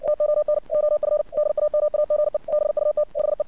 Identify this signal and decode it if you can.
Morse code, heard as a keyed tone, about 57 wpm, 600 Hz. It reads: TON OF DAMAGE BUT HE
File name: archived / CRSnet